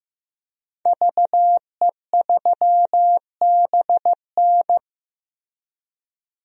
Morse code, heard as a keyed tone, about 15 wpm, 700 Hz. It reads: VE3BN
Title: DXCC-100 Call Signs - 5 Characters - Repeated 2x - 15wpm